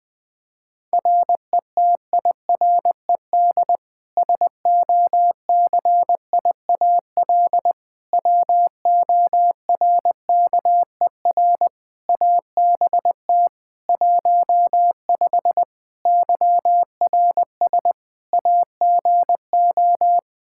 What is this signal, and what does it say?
Morse code, heard as a keyed tone, about 20 wpm, 700 Hz. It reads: RETIRED SOCIAL WORKER ABT 15 YRS AGO
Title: K1USN SST QSO Practice 20wpm